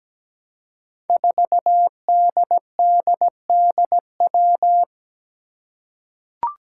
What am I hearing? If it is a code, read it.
Morse code, heard as a keyed tone, about 17 wpm, 700 Hz. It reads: 4DDDW
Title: Five Challenging Letter-Number Combinations 17wpm